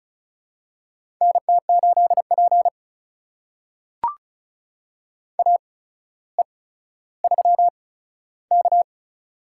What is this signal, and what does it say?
Morse code, heard as a keyed tone, about 34 wpm, spaced as for 15 wpm, 700 Hz. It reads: NT8P AE3K
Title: USA Call Signs - 4 Characters - 8x Character Spacing - 35wpm